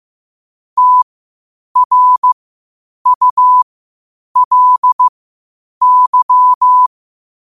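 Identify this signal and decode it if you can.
Morse code, heard as a keyed tone, about 15 wpm, spaced as for 8 wpm, 1000 Hz. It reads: TRULY
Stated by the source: Morse code at 8 wpm.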